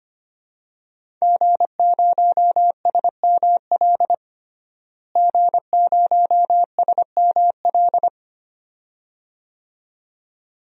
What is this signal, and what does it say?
Morse code, heard as a keyed tone, about 25 wpm, 700 Hz. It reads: G0SML G0SML
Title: DXCC-100 Call Signs - 5 Characters - Repeated 2x - 25wpm